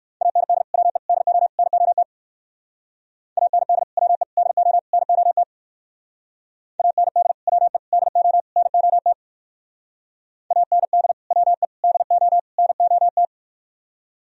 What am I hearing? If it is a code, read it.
Morse code, heard as a keyed tone, about 58 wpm, 700 Hz. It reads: AND WE DO NOT AND WE DO NOT AND WE DO NOT AND WE DO NOT